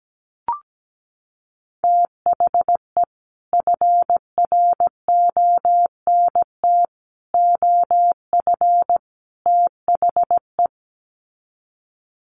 Morse code, heard as a keyed tone, about 17 wpm, 700 Hz. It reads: THE FRONT OF THE
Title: Sets of 4 Words 17wpm